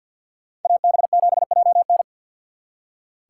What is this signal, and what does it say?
Morse code, heard as a keyed tone, about 64 wpm, 700 Hz. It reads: AB7JN